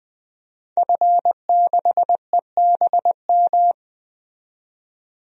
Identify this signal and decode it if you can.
Morse code, heard as a keyed tone, about 21 wpm, 700 Hz. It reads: F6EBM
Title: DXCC-100 Call Signs - 5 Characters - 20wpm